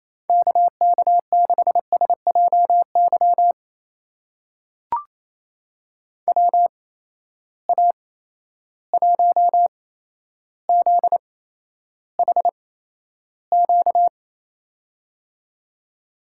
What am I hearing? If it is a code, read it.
Morse code, heard as a keyed tone, about 28 wpm, spaced as for 12 wpm, 700 Hz. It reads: KK6SJY WA1ZHQ
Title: USA Call Signs - 6 Characters - 8x Character Spacing - 28wpm